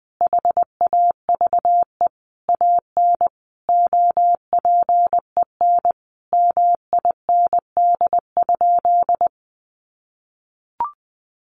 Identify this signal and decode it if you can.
Morse code, heard as a keyed tone, about 20 wpm, 700 Hz. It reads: HAVE AN OPEN MIND?
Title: Sentences from Top 1300 Words 20wpm